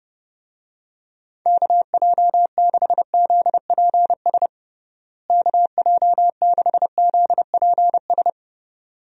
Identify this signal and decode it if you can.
Morse code, heard as a keyed tone, about 30 wpm, 700 Hz. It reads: KJ6ZPS KJ6ZPS
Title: USA Call Signs - 6 Characters - Repeated 2x - 30wpm